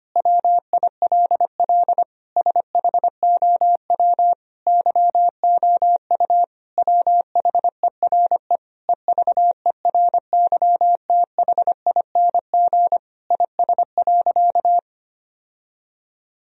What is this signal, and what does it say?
Morse code, heard as a keyed tone, about 25 wpm, 700 Hz. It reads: WILL SHOW YOU WHERE EVERYTHING IS.
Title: Sentences from Top 1000 Words - Review Part 4 - 25wpm